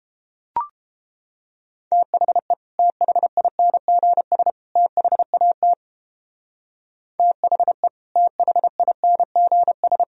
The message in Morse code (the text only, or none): THE THINGS THAT THE THINGS